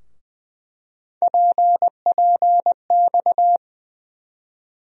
Morse code, heard as a keyed tone, about 20 wpm, 700 Hz. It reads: PPX
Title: USA Call Sign Postfixes - 3 Letters 20wpm